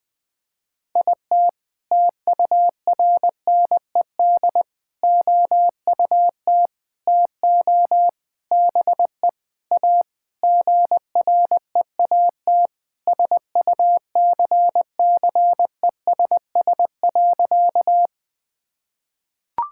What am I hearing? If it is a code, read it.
Morse code, heard as a keyed tone, about 20 wpm, 700 Hz. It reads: IT TURNED OUT TO BE A GREAT SUCCESS.
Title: Sentences with 6-Word Phrases - Spoken Beginning - 20wpm